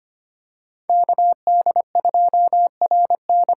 Morse code, heard as a keyed tone, about 25 wpm, 700 Hz. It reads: KD2RN
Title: USA Call Signs - 6 Characters - 8x Character Spacing - 25wpm